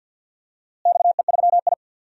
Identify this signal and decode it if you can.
Morse code, heard as a keyed tone, about 47 wpm, 700 Hz. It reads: XE3I
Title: DXCC-100 Call Signs - 4 Characters - Repeated 2x - 50wpm